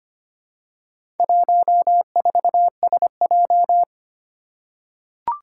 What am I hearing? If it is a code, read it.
Morse code, heard as a keyed tone, about 25 wpm, 700 Hz. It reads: 14SJ
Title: Four Challenging Letter-Number Combinations 25wpm